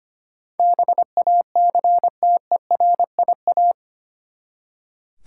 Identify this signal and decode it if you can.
Morse code, heard as a keyed tone, about 25 wpm, 700 Hz. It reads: BACTERIA
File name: Eight Letter Words - 25wpm